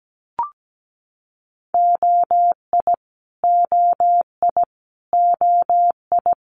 Morse code, heard as a keyed tone, about 17 wpm, 700 Hz. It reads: OI OI OI
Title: Multi-letter Phonemes Repeated 3x 17wpm